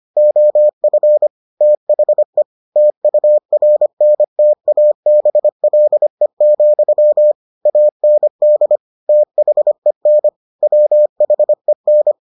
Morse code, heard as a keyed tone, about 25 wpm, 600 Hz. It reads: OF THE TURNTABLE, AND THEN WHEN